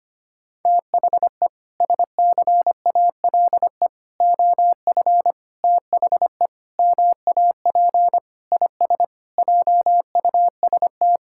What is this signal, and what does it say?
Morse code, heard as a keyed tone, about 25 wpm, 700 Hz. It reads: THE SCALE OF THE MAP IS JUST
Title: Sentences from Top 1000 Words - Review Part 5 - 25wpm